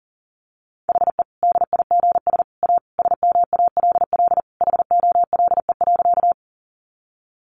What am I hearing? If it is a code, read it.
Morse code, heard as a keyed tone, about 40 wpm, 700 Hz. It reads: HE DIGS A SMALL HOLE.
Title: Easy Phrases 40wpm